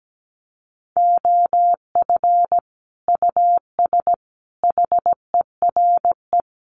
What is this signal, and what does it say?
Morse code, heard as a keyed tone, about 17 wpm, 700 Hz. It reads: OF US HERE
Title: Sets of 5 Words 17wpm